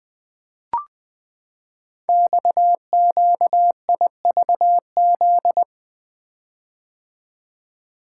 Morse code, heard as a keyed tone, about 20 wpm, 700 Hz. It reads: XQIVZ
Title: Code Groups - 5 Letters - 20wpm